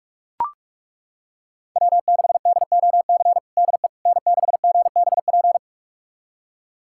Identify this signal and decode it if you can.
Morse code, heard as a keyed tone, about 41 wpm, 700 Hz. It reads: W6DOC DE N6GBP
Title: USA Callsign DE Callsign - 5 Characters - 45wpm